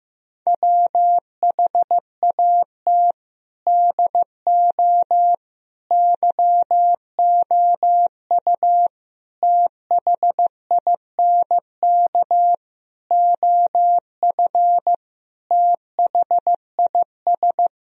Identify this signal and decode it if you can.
Morse code, heard as a keyed tone, about 15 wpm, 700 Hz. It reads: WHAT DO YOU THINK OF THIS
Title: Sentences from Top 1000 Words - Review Part 3 - 15wpm